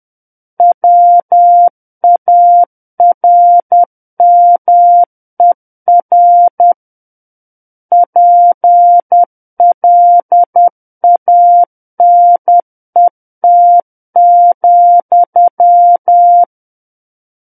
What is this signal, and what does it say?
Morse code, heard as a keyed tone, about 10 wpm, 700 Hz. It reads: WARMER PLANET,
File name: Book One / 01-Chapter 1 - 10 WPM